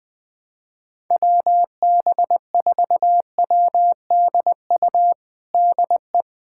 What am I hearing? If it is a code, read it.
Morse code, heard as a keyed tone, about 21 wpm, 700 Hz. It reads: WB4WDU DE
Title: USA Callsign DE Callsign - 6 Characters - 20wpm